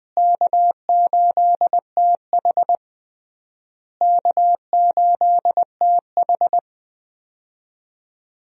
Morse code, heard as a keyed tone, about 20 wpm, 700 Hz. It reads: K8TH K8TH
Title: USA Call Signs - 4 Characters - Repeated 3x - 20wpm